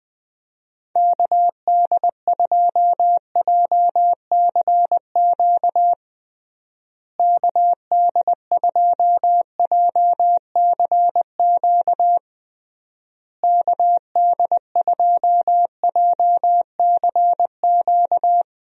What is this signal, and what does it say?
Morse code, heard as a keyed tone, about 20 wpm, 700 Hz. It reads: KD2JCQ KD2JCQ KD2JCQ
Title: USA Call Signs - 6 Characters - Repeated 3x - 20wpm